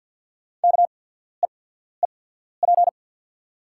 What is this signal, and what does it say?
Morse code, heard as a keyed tone, about 48 wpm, spaced as for 14 wpm, 690 Hz. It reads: KEEP
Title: 8x Character Spacing - Top 300 Words - 50wpm